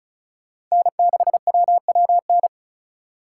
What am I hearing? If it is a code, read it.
Morse code, heard as a keyed tone, about 34 wpm, 690 Hz. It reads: N6WWN
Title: USA Call Signs - 3x Character Spacing - 35wpm